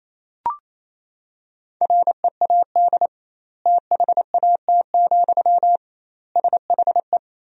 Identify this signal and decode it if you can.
Morse code, heard as a keyed tone, about 28 wpm, 700 Hz. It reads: READ THAT, SHE
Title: 2x Word-Spacing - Sentences from Top 200 Words - 28wpm